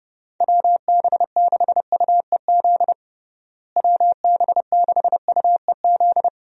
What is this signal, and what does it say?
Morse code, heard as a keyed tone, about 30 wpm, 700 Hz. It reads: WB6UEZ WB6UEZ
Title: USA Call Signs - 6 Characters - Repeated 2x - 30wpm